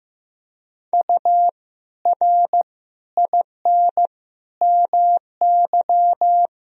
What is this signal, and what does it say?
Morse code, heard as a keyed tone, about 15 wpm, 700 Hz. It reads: U R IN MY
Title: Sentences from Top 500 Words and Abbreviations 15wpm